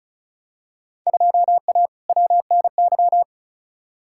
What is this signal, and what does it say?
Morse code, heard as a keyed tone, about 34 wpm, 700 Hz. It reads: 2A WNY
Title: Field Day Exchanges - 35wpm